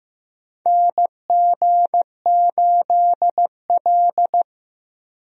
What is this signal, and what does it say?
Morse code, heard as a keyed tone, about 16 wpm, 700 Hz. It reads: NG8L
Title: USA Call Signs - 4 Characters - Repeated 3x - 15wpm